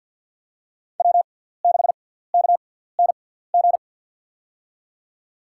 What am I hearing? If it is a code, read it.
Morse code, heard as a keyed tone, about 52 wpm, spaced as for 17 wpm, 700 Hz. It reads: W6KNG
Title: USA Call Signs - 6x Character Spacing - 50wpm